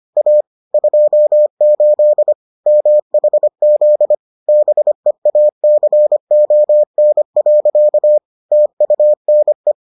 Morse code, heard as a keyed tone, about 25 wpm, 600 Hz. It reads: A 28 MHZ BEACON. TUNE